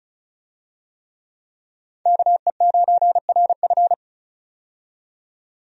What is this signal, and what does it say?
Morse code, heard as a keyed tone, about 36 wpm, 700 Hz. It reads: KE9RF
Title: USA Call Signs - 5 Characters - 35wpm